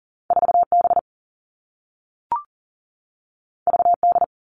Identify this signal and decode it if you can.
Morse code, heard as a keyed tone, about 42 wpm, 700 Hz. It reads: 4B VD
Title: All Cut Numbers - 2-Digit 40wpm